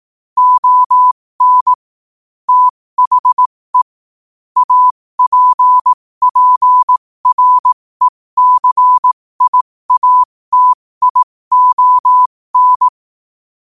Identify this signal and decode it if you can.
Morse code, heard as a keyed tone, about 18 wpm, spaced as for 15 wpm, 1000 Hz. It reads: ON THE APPRECIATION
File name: qotdfiles / 15wpm